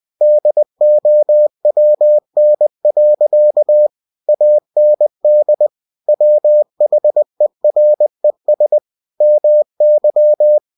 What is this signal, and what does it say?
Morse code, heard as a keyed tone, about 20 wpm, 600 Hz. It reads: DOWN. AND WHERES MY
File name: cw-books / winnie-the-pooh / winnie-20wpm-page-0053